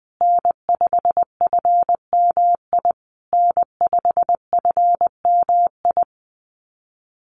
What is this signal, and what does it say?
Morse code, heard as a keyed tone, about 20 wpm, 700 Hz. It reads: N5FMI N5FMI
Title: USA Call Signs Repeated 3x 20wpm